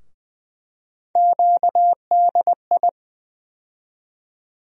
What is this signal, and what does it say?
Morse code, heard as a keyed tone, about 20 wpm, 700 Hz. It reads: QDI
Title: USA Call Sign Postfixes - 3 Letters 20wpm